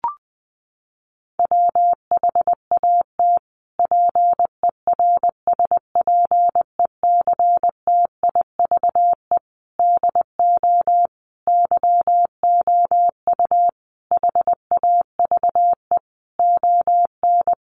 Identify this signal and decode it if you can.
Morse code, heard as a keyed tone, about 20 wpm, 700 Hz. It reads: WHAT PERSPECTIVE DO YOU HAVE ON
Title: Sentences from Top 1200 Words 20wpm